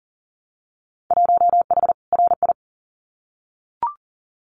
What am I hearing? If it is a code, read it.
Morse code, heard as a keyed tone, about 42 wpm, 700 Hz. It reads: 1H RI
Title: Winter Field Day Exchanges 40wpm